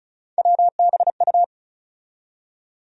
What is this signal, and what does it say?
Morse code, heard as a keyed tone, about 36 wpm, 700 Hz. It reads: WBU
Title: USA Call Sign Postfixes - 3 Letters 35wpm